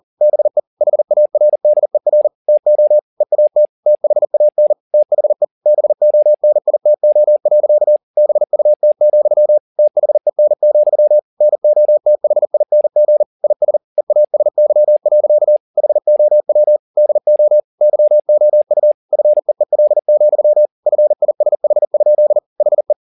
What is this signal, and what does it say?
Morse code, heard as a keyed tone, about 40 wpm, 600 Hz. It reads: BE HARDER TO EAT THAN THE BONITO. BUT, THEN, NOTHING IS EASY. HOW DO YOU FEEL, FISH? HE
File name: the old man and the sea 2 40wpm